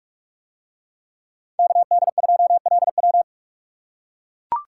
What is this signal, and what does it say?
Morse code, heard as a keyed tone, about 42 wpm, 700 Hz. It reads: KD2LW